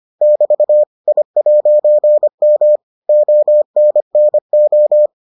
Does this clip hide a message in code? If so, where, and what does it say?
Yes: Morse code from 0.2 to 5.1 s, heard as a keyed tone, about 25 wpm, 600 Hz. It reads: <BT> I'M ONNO